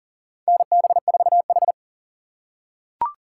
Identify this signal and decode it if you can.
Morse code, heard as a keyed tone, about 40 wpm, 700 Hz. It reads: NB4H
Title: USA Call Signs - 4 Characters - 6x Character Spacing - 40wpm